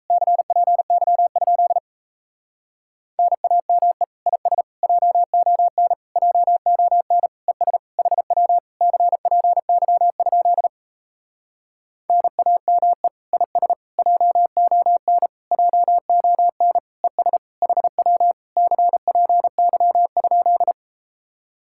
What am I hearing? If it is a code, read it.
Morse code, heard as a keyed tone, about 38 wpm, 700 Hz. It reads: CPY? NAME IS JON JON ES HW CPY? NAME IS JON JON ES HW CPY?